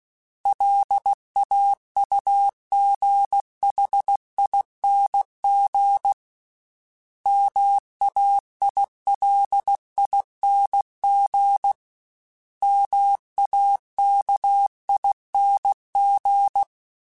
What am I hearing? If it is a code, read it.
Morse code, heard as a keyed tone, about 16 wpm, 800 Hz. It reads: LAUGHING MAILING MAKING
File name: Special INGs / Special INGs-mp3s / Sngl ing @15